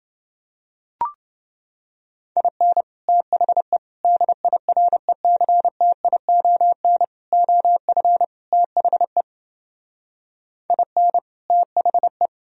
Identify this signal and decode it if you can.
Morse code, heard as a keyed tone, about 28 wpm, 700 Hz. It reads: IN THE DIRECTION OF THE IN THE